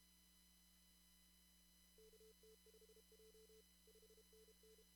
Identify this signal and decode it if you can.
Morse code, heard as a keyed tone, about 31 wpm, 440 Hz, with noise. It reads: KT5J 5NN
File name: CW / kt5j wpx cw-op-k5pi